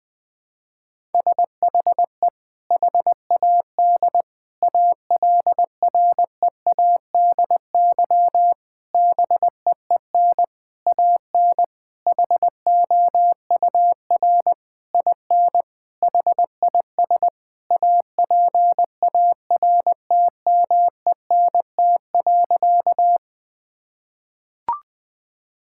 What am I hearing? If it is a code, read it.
Morse code, heard as a keyed tone, about 21 wpm, 700 Hz. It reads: SHE HAD ALREADY BEEN AN HOUR IN HIS APARTMENT.
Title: Sentences from Top 1000 Words - Review Part 6 - 20wpm